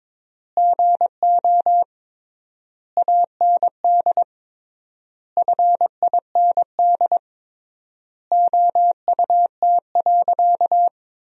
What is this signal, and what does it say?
Morse code, heard as a keyed tone, about 22 wpm, 700 Hz. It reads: GO AND FIND OUT.
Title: 3x Word-Spacing - Sentences from Top 200 Words - 22wpm